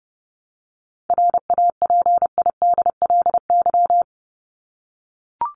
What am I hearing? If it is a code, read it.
Morse code, heard as a keyed tone, about 30 wpm, 700 Hz. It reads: RAPIDLY
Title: LY Suffix - Top 100 Words with Base Words 30wpm